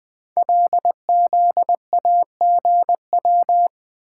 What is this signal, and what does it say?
Morse code, heard as a keyed tone, about 20 wpm, 700 Hz. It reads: LZAGW
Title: Code Groups - 5 Letter:Numbers - 20wpm